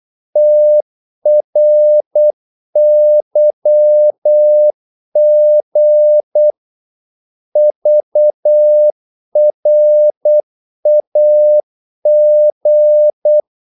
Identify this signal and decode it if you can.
Morse code, heard as a keyed tone, about 8 wpm, 600 Hz. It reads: TRYG VRAG
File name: Korte ord DK 8wpm